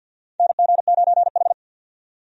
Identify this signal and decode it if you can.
Morse code, heard as a keyed tone, about 47 wpm, 700 Hz. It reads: NG9H
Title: USA Call Signs - 4 Characters - 50wpm